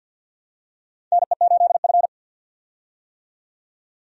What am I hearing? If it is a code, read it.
Morse code, heard as a keyed tone, about 48 wpm, 700 Hz. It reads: NE8F
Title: USA Call Signs - 4 Characters - 50wpm